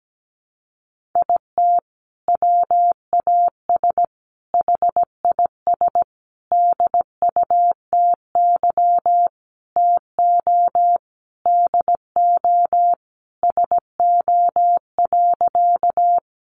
Morse code, heard as a keyed tone, about 17 wpm, 700 Hz. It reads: IT WAS HIS DUTY TO DO SO.
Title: Sentences from Top 1500 Words 17wpm